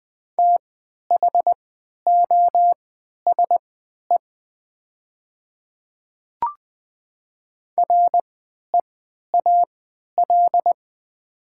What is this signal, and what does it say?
Morse code, heard as a keyed tone, about 20 wpm, spaced as for 11 wpm, 700 Hz. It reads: THOSE REAL
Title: 3x Character Spacing - Top 300 Words - 20wpm